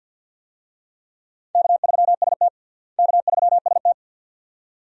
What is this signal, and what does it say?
Morse code, heard as a keyed tone, about 48 wpm, 700 Hz. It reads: K3ST K3ST
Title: USA Call Signs - 4 Characters - Repeated 2x - 50wpm